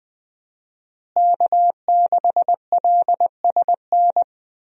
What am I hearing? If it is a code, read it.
Morse code, heard as a keyed tone, about 20 wpm, 700 Hz. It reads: K6LSN
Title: USA Call Signs - 5 Characters - Repeated 2x - 20wpm